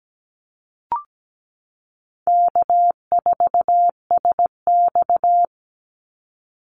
Morse code, heard as a keyed tone, about 17 wpm, 700 Hz. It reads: K4SX
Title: Easy Call Signs Repeated 2x - 17wpm